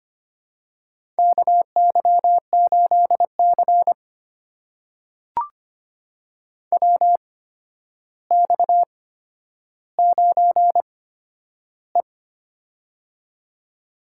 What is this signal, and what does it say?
Morse code, heard as a keyed tone, about 25 wpm, spaced as for 11 wpm, 700 Hz. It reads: KY8C WX9E
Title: USA Call Signs - 4 Characters - 8x Character Spacing - 25wpm